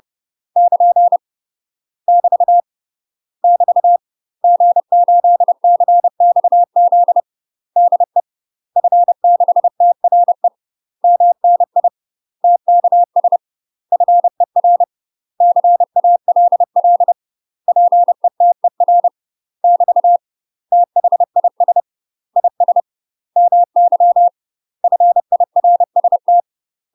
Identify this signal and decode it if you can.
Morse code, heard as a keyed tone, about 30 wpm, 700 Hz. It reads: <KN> <BT> <BT> G8CXZ DE F6TRE MNI TKS FER CALL PETER <BT> THIS IS MY FIRST
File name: UK MASTER QSOs TRAINING / 30WPM / UK MASTER QSOs 0046